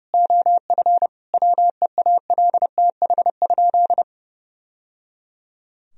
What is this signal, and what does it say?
Morse code, heard as a keyed tone, about 30 wpm, 700 Hz. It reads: OF WEALTH?
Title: Sentences from Top 1000 Words - Review Part 4 - 30wpm